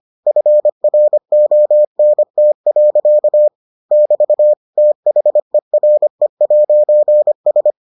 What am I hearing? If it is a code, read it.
Morse code, heard as a keyed tone, about 25 wpm, 600 Hz. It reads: FRONT. <BT> THERE'S